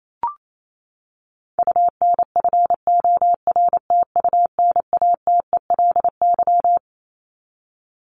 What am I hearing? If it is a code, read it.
Morse code, heard as a keyed tone, about 28 wpm, 700 Hz. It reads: UNFORTUNATELY
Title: Top 1751-2000 Words 28wpm